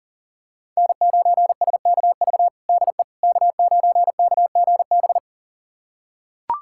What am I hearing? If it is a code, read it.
Morse code, heard as a keyed tone, about 40 wpm, 700 Hz. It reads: N9SKV DE K9KGB
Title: USA Callsign DE Callsign - 5 Characters - 40wpm